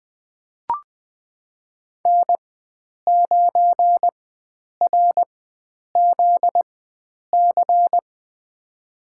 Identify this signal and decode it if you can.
Morse code, heard as a keyed tone, about 20 wpm, spaced as for 9 wpm, 700 Hz. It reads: N9RZC
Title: USA Call Signs - 5 Characters - 4x Character Spacing - 20wpm